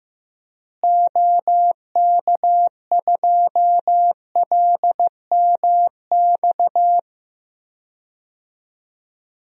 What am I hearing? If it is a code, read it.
Morse code, heard as a keyed tone, about 15 wpm, 700 Hz. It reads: OK2LMX
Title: DXCC-100 Call Signs - 6 Characters - Repeated 3x - 15wpm